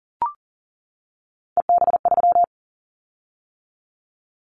Morse code, heard as a keyed tone, about 40 wpm, 700 Hz. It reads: EB3
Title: Callsign Prefixes for Top 25 DXCC - 40wpm